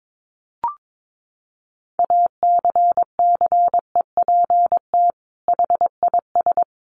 Morse code, heard as a keyed tone, about 22 wpm, 700 Hz. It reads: ACCEPT HIS
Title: Sentences from Top 1300 Words 22wpm